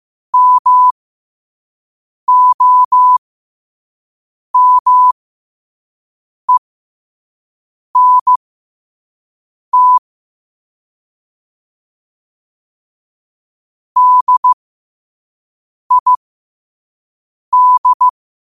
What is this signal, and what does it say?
Morse code, heard as a keyed tone, about 15 wpm, spaced as for 5 wpm, 1000 Hz. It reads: MOMENT DID
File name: 5wpm